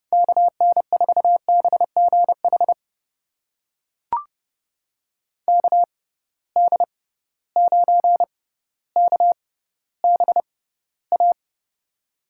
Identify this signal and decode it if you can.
Morse code, heard as a keyed tone, about 32 wpm, spaced as for 16 wpm, 700 Hz. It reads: KN4BGH KD9KBA
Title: USA Call Signs - 6 Characters - 6x Character Spacing - 30wpm